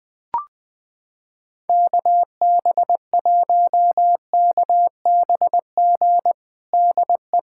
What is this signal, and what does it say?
Morse code, heard as a keyed tone, about 20 wpm, 700 Hz. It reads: KB1KBG DE
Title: USA Callsign DE Callsign 20wpm